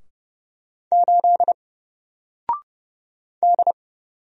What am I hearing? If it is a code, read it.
Morse code, heard as a keyed tone, about 29 wpm, 700 Hz. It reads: 8 D
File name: Single-Letter-Number-30wpm